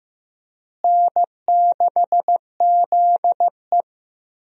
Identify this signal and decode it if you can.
Morse code, heard as a keyed tone, about 15 wpm, 700 Hz. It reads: N6ZE
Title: USA Call Signs - 4 Characters - Repeated 3x - 15wpm